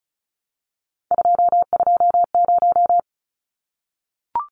Morse code, heard as a keyed tone, about 34 wpm, 700 Hz. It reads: 220
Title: Numbers - 3-Digit 35wpm